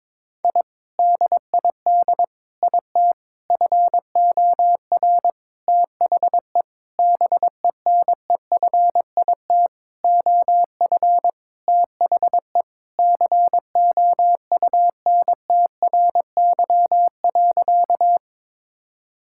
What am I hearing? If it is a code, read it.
Morse code, heard as a keyed tone, about 22 wpm, 700 Hz. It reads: I DID IT FOR THE BENEFIT OF THE COUNTRY.
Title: Sentences from Top 1000 Words - Review Part 6 - 22wpm